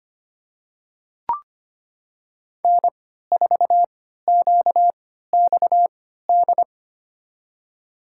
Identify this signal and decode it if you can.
Morse code, heard as a keyed tone, about 25 wpm, spaced as for 14 wpm, 700 Hz. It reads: N4QXD
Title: USA Call Signs - 5 Characters - 3x Character Spacing - 25wpm